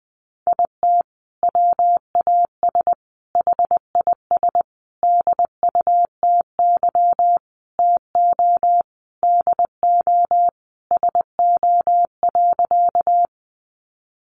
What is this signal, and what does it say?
Morse code, heard as a keyed tone, about 20 wpm, 700 Hz. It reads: IT WAS HIS DUTY TO DO SO.
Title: Sentences from Top 1500 Words 20wpm